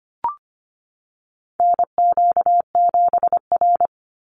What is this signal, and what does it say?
Morse code, heard as a keyed tone, about 25 wpm, 700 Hz. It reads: NQ7R
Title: NA SOTA Chaser Callsigns 25wpm